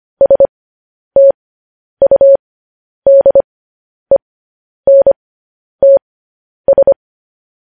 Morse code, heard as a keyed tone, about 25 wpm, spaced as for 10 wpm, 550 Hz. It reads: STUDENTS
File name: podcast / News headlines-1x-25x10